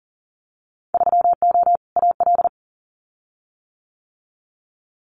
Morse code, heard as a keyed tone, about 40 wpm, 700 Hz. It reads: 3O AL
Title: Winter Field Day Exchanges 40wpm